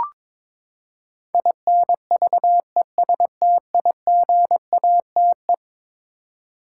Morse code, heard as a keyed tone, about 22 wpm, 700 Hz. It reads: INVESTIGATE
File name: practice / 22wpm / Eleven Letter Words - 22wpm